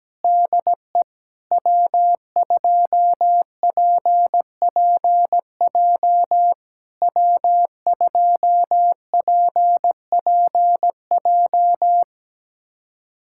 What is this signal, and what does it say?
Morse code, heard as a keyed tone, about 17 wpm, 700 Hz. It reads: DE W2PPJ W2PPJ
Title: SOTA QSO Practice 17wpm